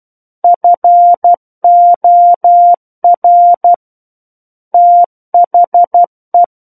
Morse code, heard as a keyed tone, about 12 wpm, 700 Hz. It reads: FOR THE
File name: Book One / 03-Chapter 3 - 12 WPM